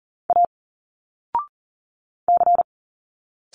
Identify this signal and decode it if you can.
Morse code, heard as a keyed tone, about 39 wpm, 700 Hz. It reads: A C
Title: Single Letters 40wpm